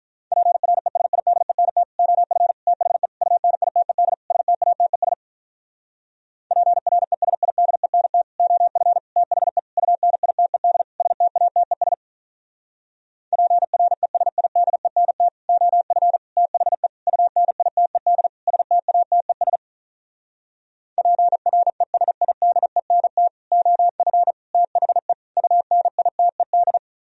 Morse code, heard as a keyed tone, about 45 wpm, 700 Hz. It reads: PRESIDENT OF THE UNITED STATES PRESIDENT OF THE UNITED STATES PRESIDENT OF THE UNITED STATES PRESIDENT OF THE UNITED